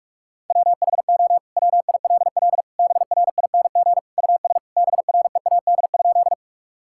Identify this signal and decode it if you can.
Morse code, heard as a keyed tone, about 46 wpm, 700 Hz. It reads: WHO WILL BRING US BREAD?
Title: Sentences from Top 1600 Words 45wpm